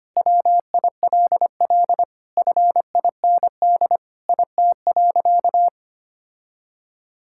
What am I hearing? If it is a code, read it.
Morse code, heard as a keyed tone, about 25 wpm, 700 Hz. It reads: WILL FIND IT.
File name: practice / 25wpm / Sentences with 3-Word Phrases - 25wpm